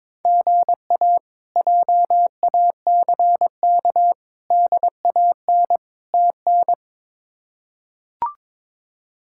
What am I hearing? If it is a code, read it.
Morse code, heard as a keyed tone, about 22 wpm, 700 Hz. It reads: GA JACK DAN TN
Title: K1USN SST QSO Practice - 22wpm